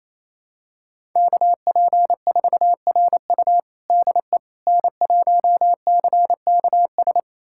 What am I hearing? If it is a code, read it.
Morse code, heard as a keyed tone, about 28 wpm, 700 Hz. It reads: KP4RU DE N1CKS
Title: USA Callsign DE Callsign - 5 Characters - 28wpm